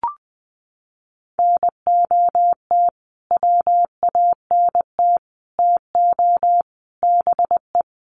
Morse code, heard as a keyed tone, about 20 wpm, 700 Hz. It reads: NOT WANT TO BE
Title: Sets of 5 Words Encore 20wpm